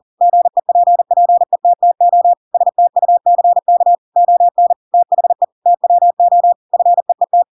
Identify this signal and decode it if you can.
Morse code, heard as a keyed tone, about 40 wpm, 700 Hz. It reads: GEPPETTO STUCK ON THE TWO FEET
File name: Pinocchio / Pinocchio ch8 40 40 ik1hge